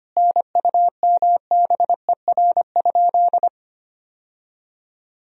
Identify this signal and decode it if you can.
Morse code, heard as a keyed tone, about 25 wpm, 700 Hz. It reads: NUMBER?
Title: Sentences from Top 1000 Words - Review Part 1 - 25wpm